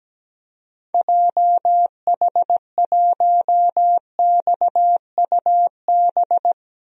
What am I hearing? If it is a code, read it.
Morse code, heard as a keyed tone, about 17 wpm, 700 Hz. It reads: JH1XUB
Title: DXCC-100 Call Signs - 6 Characters - 17wpm